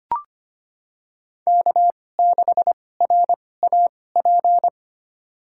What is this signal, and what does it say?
Morse code, heard as a keyed tone, about 25 wpm, spaced as for 18 wpm, 700 Hz. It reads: K6RAP
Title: USA Call Signs - 5 Characters - 2x Character Spacing - 25wpm